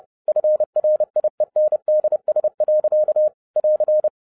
Morse code, heard as a keyed tone, about 30 wpm, 610 Hz. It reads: FRIENDS. <AR>
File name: 2012-01-mp3 / n12-I DIDNT MISS THE SOLAR CYCLE 24-30wpm